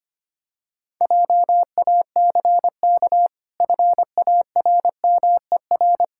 Morse code, heard as a keyed tone, about 25 wpm, 700 Hz. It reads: JACK FARMER
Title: Pangram Challenge v2 - 25wpm